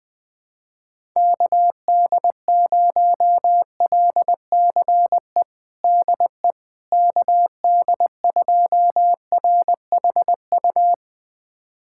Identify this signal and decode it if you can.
Morse code, heard as a keyed tone, about 21 wpm, 700 Hz. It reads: KD0LCE DE KD2RHU
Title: USA Callsign DE Callsign - 6 Characters - 20wpm